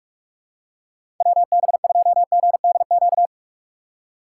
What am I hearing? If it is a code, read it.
Morse code, heard as a keyed tone, about 44 wpm, 690 Hz. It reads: WB2GDQ
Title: USA Call Signs - 6 Characters - 3x Character Spacing - 45wpm